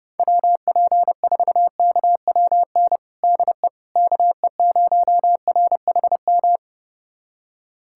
Morse code, heard as a keyed tone, about 30 wpm, 700 Hz. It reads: WP4KWN DE KE0RHM
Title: USA Callsign DE Callsign - 6 Characters - 30wpm